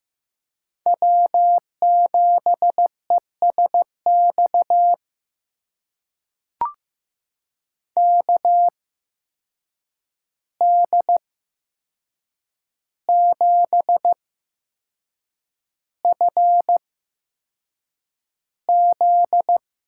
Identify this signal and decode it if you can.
Morse code, heard as a keyed tone, about 15 wpm, spaced as for 6 wpm, 700 Hz. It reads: W7ESX KD7FZ
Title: USA Call Signs - 5 Characters - 8x Character Spacing - 15wpm